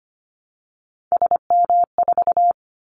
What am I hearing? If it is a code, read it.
Morse code, heard as a keyed tone, about 25 wpm, 700 Hz. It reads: SM4
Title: Callsign Prefixes for Top 25 DXCC - Repeated 3x - 25wpm